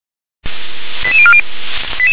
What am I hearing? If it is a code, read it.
Five-tone selective call (ZVEI), heard as quick runs of five tones, tones of about 70 ms. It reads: 88040